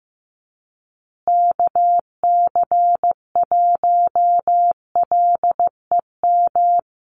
Morse code, heard as a keyed tone, about 15 wpm, 700 Hz. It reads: KC1LEM
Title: USA Call Signs Repeated 3x 15wpm